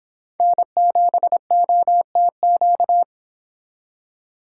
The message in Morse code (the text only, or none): N7OTQ